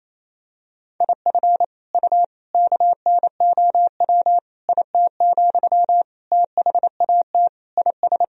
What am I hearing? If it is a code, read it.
Morse code, heard as a keyed tone, about 28 wpm, 700 Hz. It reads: IF U KNOW IT, THAT IS
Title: Sentences from Top 400 Words and Abbreviations 28wpm